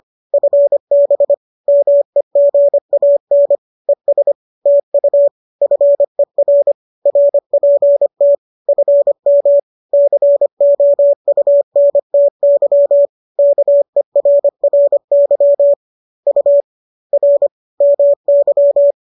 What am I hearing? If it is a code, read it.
Morse code, heard as a keyed tone, about 25 wpm, 570 Hz. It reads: FB MEGAN ES TU FER RPT FM COUNTY KERRY U R MY